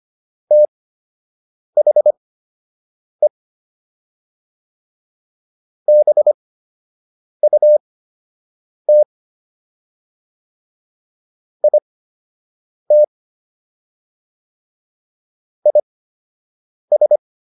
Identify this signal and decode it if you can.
Morse code, heard as a keyed tone, about 25 wpm, spaced as for 7 wpm, 600 Hz. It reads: THE BUT IT IS